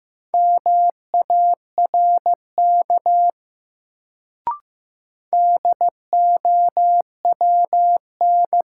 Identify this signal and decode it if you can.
Morse code, heard as a keyed tone, about 15 wpm, 700 Hz. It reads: MARK DOWN
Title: Sentences from Top 300 Words - New Word Drill - 15wpm